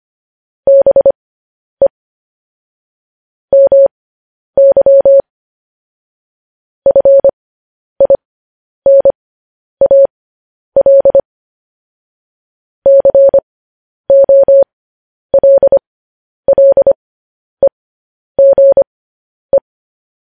Morse code, heard as a keyed tone, about 25 wpm, spaced as for 10 wpm, 550 Hz. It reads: BE MY FINAL COLLEGE